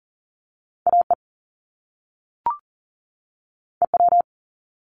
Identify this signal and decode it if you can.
Morse code, heard as a keyed tone, about 40 wpm, 700 Hz. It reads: AE EW
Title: Top Two Letter Combinations in Words 40wpm